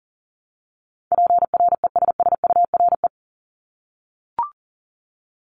Morse code, heard as a keyed tone, about 37 wpm, 700 Hz. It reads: PRESSURE
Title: Multi-letter Phonemes Repeated 3x 40wpm